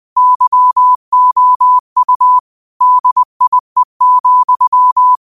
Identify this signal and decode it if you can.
Morse code, heard as a keyed tone, about 20 wpm, 1000 Hz. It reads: YOU DIE,
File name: qotdfiles / 20wpm